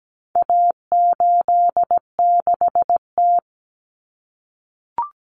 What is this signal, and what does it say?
Morse code, heard as a keyed tone, about 17 wpm, 700 Hz. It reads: A86T
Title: Common Cut Numbers - 4-Digit 17wpm